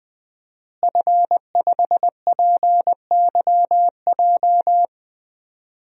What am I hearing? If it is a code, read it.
Morse code, heard as a keyed tone, about 21 wpm, 700 Hz. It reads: F5PYJ
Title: DXCC-100 Call Signs - 5 Characters - Repeated 2x - 20wpm